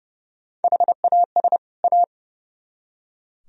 Morse code, heard as a keyed tone, about 35 wpm, 700 Hz. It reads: HAS A
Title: Speed-Racing - Sets of 2 Words 30wpm